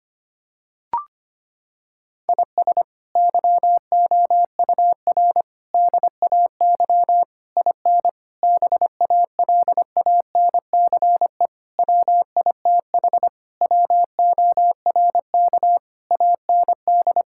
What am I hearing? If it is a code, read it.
Morse code, heard as a keyed tone, about 25 wpm, 700 Hz. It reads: IS YOUR DAY IN BALANCE WITH WORK AND
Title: Sentences from Top 1000 Words - Review Part 6 - 25wpm